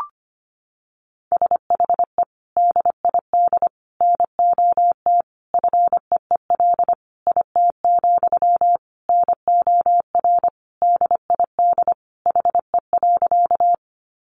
Morse code, heard as a keyed tone, about 25 wpm, 700 Hz. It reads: SHE DID NOT FEEL IT, NOR DID HE.
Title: Sentences from Top 1500 Words 25wpm